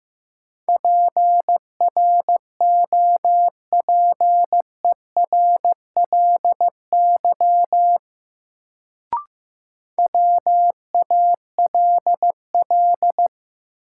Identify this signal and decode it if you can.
Morse code, heard as a keyed tone, about 15 wpm, 700 Hz. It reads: PROPERLY WALL
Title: Sentences from Top 1100 Words - New Word Drill Alt - 15wpm